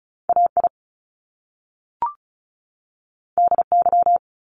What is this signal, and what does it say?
Morse code, heard as a keyed tone, about 36 wpm, 700 Hz. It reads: AI DY
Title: Top Two Letter Combinations in Words 35wpm